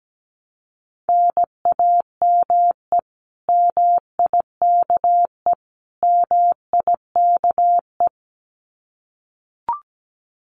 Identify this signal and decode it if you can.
Morse code, heard as a keyed tone, about 17 wpm, 700 Hz. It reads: NAME MIKE MIKE
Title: Common QSO Phrases Encore 17wpm